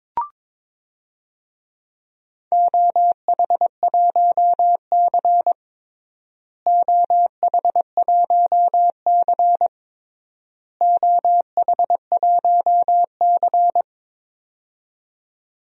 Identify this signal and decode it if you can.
Morse code, heard as a keyed tone, about 22 wpm, 700 Hz. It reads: OH1C OH1C OH1C
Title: DXCC-100 Call Signs - 4 Characters - Repeated 3x - 22wpm